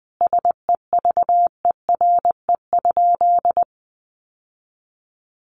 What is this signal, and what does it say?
Morse code, heard as a keyed tone, about 20 wpm, 700 Hz. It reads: SEVERE?
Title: Sentences from Top 1400 Words 20wpm